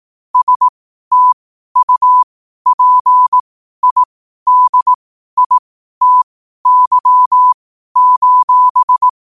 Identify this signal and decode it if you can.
Morse code, heard as a keyed tone, about 18 wpm, spaced as for 12 wpm, 1000 Hz. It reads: STUPIDITY:
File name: qotdfiles / 12wpm